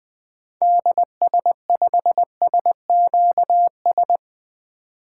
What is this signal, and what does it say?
Morse code, heard as a keyed tone, about 20 wpm, 700 Hz. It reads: DS5SQS
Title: 2023 SOTA Activator Call Signs - 20wpm